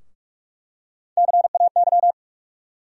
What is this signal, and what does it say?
Morse code, heard as a keyed tone, about 46 wpm, 700 Hz. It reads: CAY
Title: USA Call Sign Postfixes - 3 Letters 45wpm